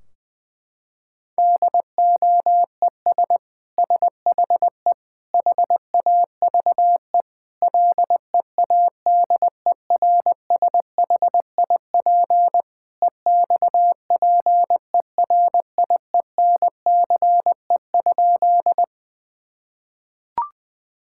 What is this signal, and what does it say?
Morse code, heard as a keyed tone, about 21 wpm, 700 Hz. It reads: DOES SHE HAVE LEADERSHIP EXPERIENCE?
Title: Sentences from Top 1600 Words 20wpm